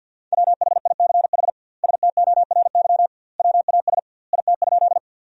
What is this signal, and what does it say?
Morse code, heard as a keyed tone, about 50 wpm, 700 Hz. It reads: WHICH STORY WAS IT?
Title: 2x Word-Spacing - Sentences from Top 300 Words - 50wpm